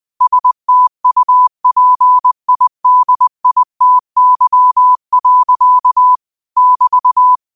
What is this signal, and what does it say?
Morse code, heard as a keyed tone, about 20 wpm, 1000 Hz. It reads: STUPIDITY. <BT>